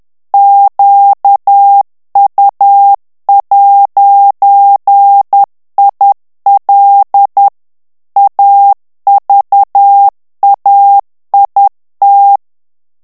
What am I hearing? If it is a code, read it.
Morse code, heard as a keyed tone, about 10 wpm, 800 Hz. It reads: QU'IL AVAIT
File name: morse mp3 / FRANCAIS TEXTE CLAIR 2 VITESSE 10 MOTS MINUTES